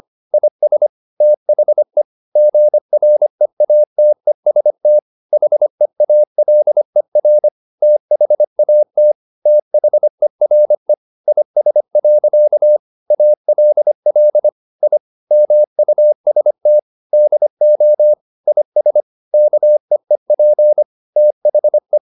Morse code, heard as a keyed tone, about 25 wpm, 600 Hz. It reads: IS THE GREATEST HEALER THAT THERE IS. ALL I MUST DO IS KEEP THE